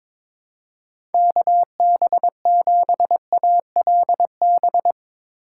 Morse code, heard as a keyed tone, about 22 wpm, 700 Hz. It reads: KB7ALB
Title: USA Call Signs - 3x Character Spacing - 22wpm